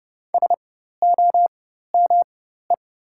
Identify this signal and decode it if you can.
Morse code, heard as a keyed tone, about 31 wpm, spaced as for 14 wpm, 700 Hz. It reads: SOME
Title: 4x Character Spacing - Top 300 Words - 30wpm